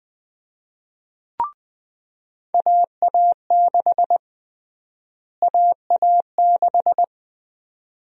Morse code, heard as a keyed tone, about 20 wpm, 700 Hz. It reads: AA6 AA6
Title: USA Call Sign Prefixes - Repeated 2x - 20wpm